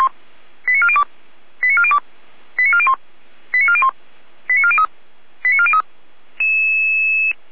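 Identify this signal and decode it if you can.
Five-tone selective call (ZVEI), heard as quick runs of five tones, tones of about 70 ms. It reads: 89402 89402 89401 89401 89403 89403
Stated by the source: ZVEI five-tone selective call received by SDR.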